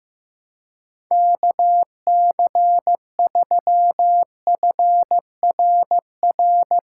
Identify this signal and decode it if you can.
Morse code, heard as a keyed tone, about 15 wpm, 700 Hz. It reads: KC3FRR
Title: USA Call Signs - 6 Characters - 3x Character Spacing - 15wpm